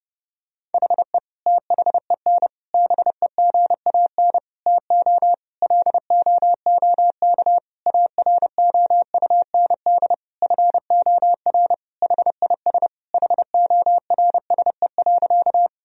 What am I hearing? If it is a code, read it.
Morse code, heard as a keyed tone, about 30 wpm, 700 Hz. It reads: HE THEN BEGAN TO LOOK AROUND FOR HIS HORSE.
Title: Sentences from Top 1000 Words - Review Part 3 - 30wpm